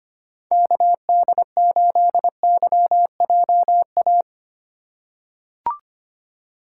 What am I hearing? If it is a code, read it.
Morse code, heard as a keyed tone, about 25 wpm, 700 Hz. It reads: KD8YJA
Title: USA Call Signs - 2x Character Spacing - 25wpm